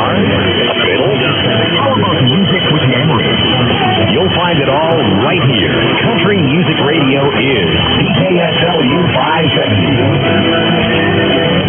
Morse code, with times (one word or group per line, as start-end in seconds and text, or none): none